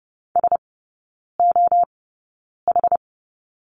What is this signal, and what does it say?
Morse code, heard as a keyed tone, about 30 wpm, spaced as for 9 wpm, 700 Hz. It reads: SOH
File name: Lesson 17e - Letter u - Three Characters 30wpm